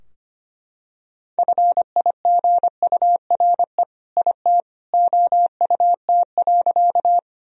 Morse code, heard as a keyed tone, about 25 wpm, 700 Hz. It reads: FIGURE IT OUT.
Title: Sentences from Top 1000 Words - Review Part 2 - 25wpm